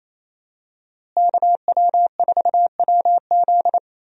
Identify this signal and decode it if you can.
Morse code, heard as a keyed tone, about 27 wpm, 700 Hz. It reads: KW4WZ
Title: USA Call Signs - 5 Characters - Repeated 2x - 28wpm